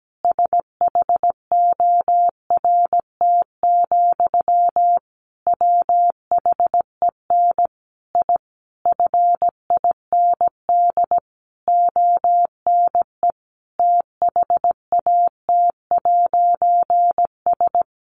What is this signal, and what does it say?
Morse code, heard as a keyed tone, about 17 wpm, 700 Hz. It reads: SHORT, WHEN I FIND ONE THAT'S
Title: A Sound Of Thunder 17wpm